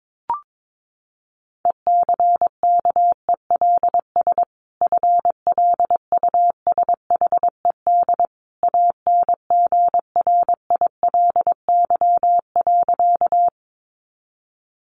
Morse code, heard as a keyed tone, about 22 wpm, 700 Hz. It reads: ECKELS FLUSHED ANGRILY.
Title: A Sound Of Thunder 22wpm